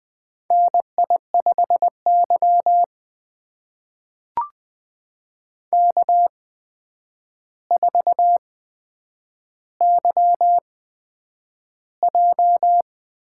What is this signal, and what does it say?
Morse code, heard as a keyed tone, about 20 wpm, spaced as for 9 wpm, 700 Hz. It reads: NI5Y K4YJ
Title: USA Call Signs - 4 Characters - 8x Character Spacing - 20wpm